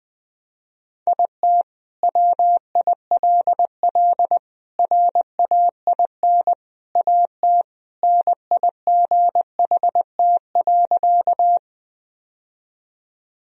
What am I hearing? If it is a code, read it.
Morse code, heard as a keyed tone, about 21 wpm, 700 Hz. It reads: IT WILL RAIN AT NIGHT.
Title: Sentences from Top 1000 Words - Review Part 1 - 20wpm